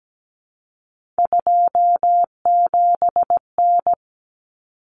Morse code, heard as a keyed tone, about 17 wpm, 700 Hz. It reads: 27N
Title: Common Cut Numbers - 3-Digit 17wpm